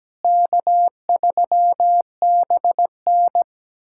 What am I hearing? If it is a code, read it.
Morse code, heard as a keyed tone, about 17 wpm, 700 Hz. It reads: K3BN
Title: USA Callsign DE Callsign - 4 Characters - 17wpm